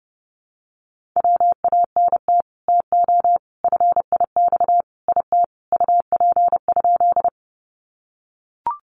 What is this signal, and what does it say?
Morse code, heard as a keyed tone, about 30 wpm, 700 Hz. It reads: WANT TO FIX IT UP?
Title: Sentences from Top 1400 Words 30wpm